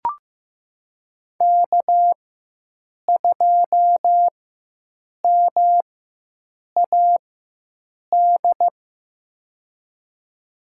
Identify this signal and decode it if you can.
Morse code, heard as a keyed tone, about 15 wpm, spaced as for 7 wpm, 700 Hz. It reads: K2MAD
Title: USA Call Signs - 5 Characters - 4x Character Spacing - 15wpm